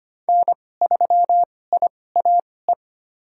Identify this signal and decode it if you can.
Morse code, heard as a keyed tone, about 25 wpm, spaced as for 18 wpm, 700 Hz. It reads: N3IAE
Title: USA Call Signs - 5 Characters - 2x Character Spacing - 25wpm